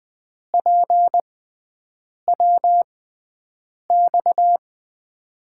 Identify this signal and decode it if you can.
Morse code, heard as a keyed tone, about 20 wpm, spaced as for 7 wpm, 700 Hz. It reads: PWX
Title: USA Call Signs - 6 Characters - 6x Character Spacing - 20wpm